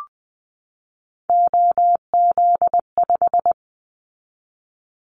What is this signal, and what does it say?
Morse code, heard as a keyed tone, about 20 wpm, 700 Hz. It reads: OZ5
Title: Callsign Prefixes for Top 100 DXCC - 20wpm